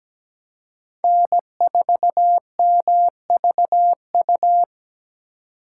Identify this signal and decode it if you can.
Morse code, heard as a keyed tone, about 20 wpm, 700 Hz. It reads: N4MVU
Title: Speed Racing - USA 1x3 Call Signs 17wpm